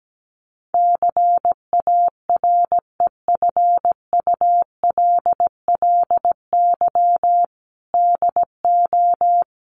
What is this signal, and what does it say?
Morse code, heard as a keyed tone, about 17 wpm, 700 Hz. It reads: CAREFULLY DO
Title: Sentences from Top 1300 Words 17wpm